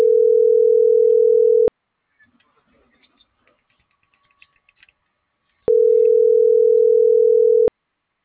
Ringback tone (North America), 440 + 480 Hz.